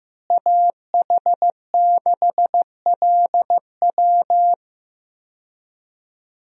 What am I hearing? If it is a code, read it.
Morse code, heard as a keyed tone, about 15 wpm, 700 Hz. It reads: AH6LW
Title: USA Call Signs - 5 Characters - Repeated 2x - 15wpm